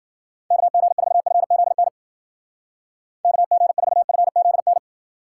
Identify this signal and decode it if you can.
Morse code, heard as a keyed tone, about 49 wpm, 700 Hz. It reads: KG4FZN KG4FZN
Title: Speed Racing - USA 2x3 Call Signs 40wpm